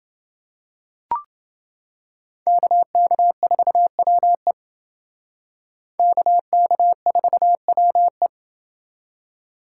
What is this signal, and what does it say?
Morse code, heard as a keyed tone, about 29 wpm, 700 Hz. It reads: KK4WE KK4WE